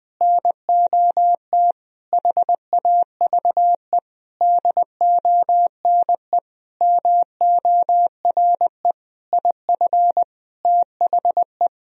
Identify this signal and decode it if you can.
Morse code, heard as a keyed tone, about 20 wpm, 700 Hz. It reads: NOT HAVE DONE MORE IF THE
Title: Sentences from Top 1000 Words - Review Part 2 - 20wpm